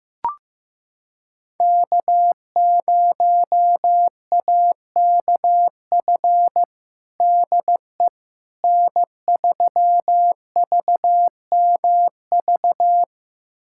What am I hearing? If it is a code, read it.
Morse code, heard as a keyed tone, about 15 wpm, 700 Hz. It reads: K0AKF DE N3VMV
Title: USA Callsign DE Callsign - 5 Characters - 15wpm